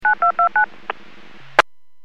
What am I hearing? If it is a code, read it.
DTMF, heard as telephone touch tones, tones of about 110 ms. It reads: 9229
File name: WPS / 00055 15-58 WPS 451.25 WPS